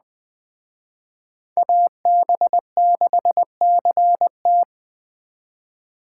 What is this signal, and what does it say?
Morse code, heard as a keyed tone, about 20 wpm, 700 Hz. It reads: AB6CT
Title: Speed Racing - USA 2x2 Call Signs 17wpm